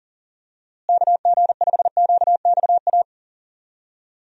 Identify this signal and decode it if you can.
Morse code, heard as a keyed tone, about 40 wpm, 700 Hz. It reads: KG5QXA
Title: USA Call Signs - 6 Characters - Repeated 3x - 40wpm